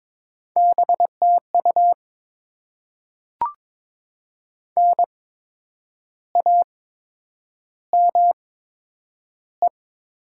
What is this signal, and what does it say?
Morse code, heard as a keyed tone, about 22 wpm, spaced as for 8 wpm, 700 Hz. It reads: BTU NAME
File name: Top 20 QSO Elements - 8x Character Spacing - 22wpm